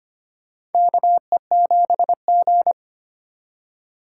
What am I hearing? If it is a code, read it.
Morse code, heard as a keyed tone, about 25 wpm, 700 Hz. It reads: KE7G
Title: USA Call Signs - 4 Characters - 4x Character Spacing - 25wpm